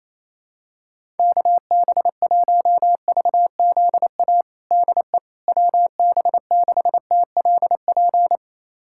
Morse code, heard as a keyed tone, about 28 wpm, 700 Hz. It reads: KB1VZA DE WB6TLP
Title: USA Callsign DE Callsign - 6 Characters - 28wpm